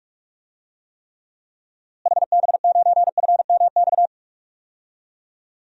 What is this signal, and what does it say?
Morse code, heard as a keyed tone, about 43 wpm, 700 Hz. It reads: HB9FMX
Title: DXCC-100 Call Signs - 6 Characters - Repeated 2x - 45wpm